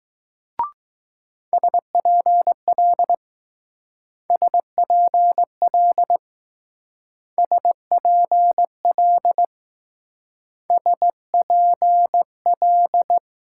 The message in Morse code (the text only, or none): SPL SPL SPL SPL